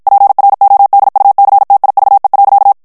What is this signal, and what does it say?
Morse code, heard as a keyed tone, about 56 wpm, 800 Hz. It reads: PRODUCTIVE.
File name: morse mp3 / ANGLAIS texte clair 1 vitesse 50 mots minute